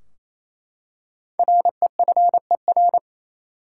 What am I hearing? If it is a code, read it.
Morse code, heard as a keyed tone, about 28 wpm, 700 Hz. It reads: REFER
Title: Five Letter Words - 28wpm